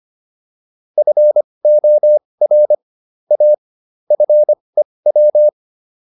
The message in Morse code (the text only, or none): FOR A FEW